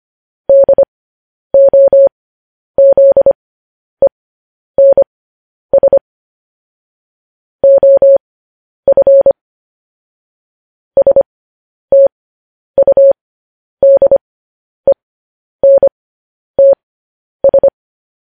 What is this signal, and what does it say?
Morse code, heard as a keyed tone, about 25 wpm, spaced as for 10 wpm, 550 Hz. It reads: DOZENS OF STUDENTS